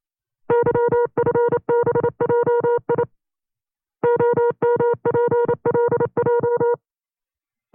Morse code, heard as a keyed tone, about 28 wpm, 480 Hz. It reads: YFBJI OMPLJ
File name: FreeFiles / CW-practice / 25WPM-side2